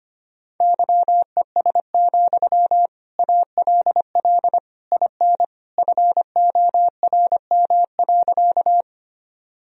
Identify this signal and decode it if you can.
Morse code, heard as a keyed tone, about 25 wpm, 700 Hz. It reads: YES, ALL IN FORM.
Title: Sentences from Top 1000 Words - Review Part 1 - 25wpm